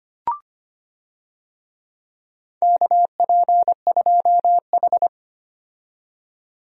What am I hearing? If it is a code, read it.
Morse code, heard as a keyed tone, about 25 wpm, 700 Hz. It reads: KP2H
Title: DXCC-100 Call Signs - 4 Characters - 25wpm